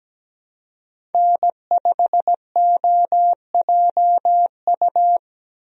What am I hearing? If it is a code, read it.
Morse code, heard as a keyed tone, about 17 wpm, 700 Hz. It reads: N5OJU
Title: USA Call Signs - 5 Characters - Repeated 2x - 17wpm